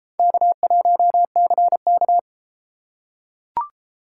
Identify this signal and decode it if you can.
Morse code, heard as a keyed tone, about 33 wpm, 700 Hz. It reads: K1CK